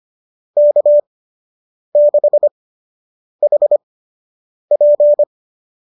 Morse code, heard as a keyed tone, about 25 wpm, spaced as for 8 wpm, 570 Hz. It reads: K6HP
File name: qso003 08